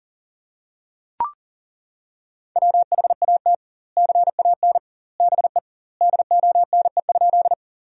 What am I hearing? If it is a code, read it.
Morse code, heard as a keyed tone, about 40 wpm, 700 Hz. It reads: WHAT CAN BE DONE?
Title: 2x Word-Spacing - Sentences from Top 500 Words - 40wpm